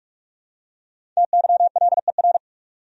Morse code, heard as a keyed tone, about 46 wpm, 700 Hz. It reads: TYLER
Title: USA Common Names - 45wpm